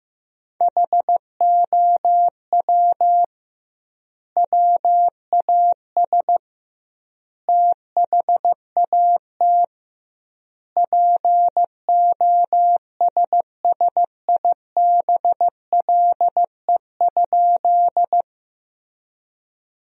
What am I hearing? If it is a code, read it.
Morse code, heard as a keyed tone, about 15 wpm, 700 Hz. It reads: HOW WAS THAT POSSIBLE?
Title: 2x Word-Spacing - Sentences from Top 500 Words - 15wpm